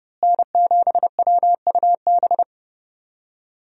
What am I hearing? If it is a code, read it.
Morse code, heard as a keyed tone, about 30 wpm, 700 Hz. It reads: N7WUB
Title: USA Call Signs - 5 Characters - Repeated 2x - 30wpm